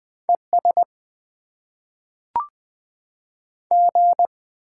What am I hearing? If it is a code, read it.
Morse code, heard as a keyed tone, about 20 wpm, 700 Hz. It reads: ES G
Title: Top 90 QSO Elements - 6x Character Spacing - 20wpm